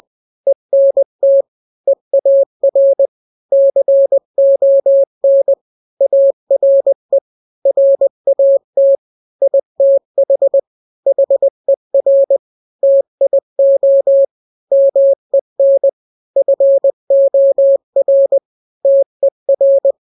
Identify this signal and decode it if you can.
Morse code, heard as a keyed tone, about 20 wpm, 550 Hz. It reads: ENT EAR CON ARE RAT ITH HER TIO MEN FOR TER
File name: n-grams / 3-grams 20 18 WPM